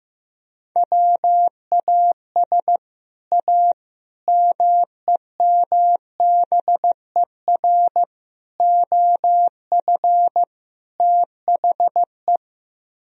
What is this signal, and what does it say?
Morse code, heard as a keyed tone, about 15 wpm, 700 Hz. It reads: WAS A MEMBER OF THE
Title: Speed-Racing - Sets of 5 Words 15wpm